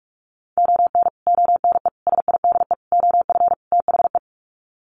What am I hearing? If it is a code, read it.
Morse code, heard as a keyed tone, about 44 wpm, 700 Hz. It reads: ON ONE SIDE OF THE
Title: Sets of 5 Words 45wpm